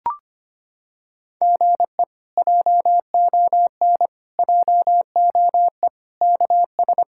Morse code, heard as a keyed tone, about 25 wpm, 700 Hz. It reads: GE JON JOE KS
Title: K1USN SST QSO Practice - 25wpm